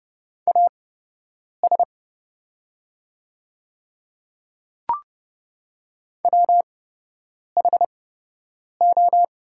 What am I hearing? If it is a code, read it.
Morse code, heard as a keyed tone, about 30 wpm, spaced as for 8 wpm, 700 Hz. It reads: AS WHO
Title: 8x Character Spacing - Top 100 Words - 30wpm